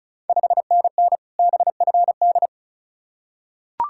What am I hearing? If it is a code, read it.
Morse code, heard as a keyed tone, about 36 wpm, 700 Hz. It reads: 5NN BFD
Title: Ontario QSO Party Exchange 35wpm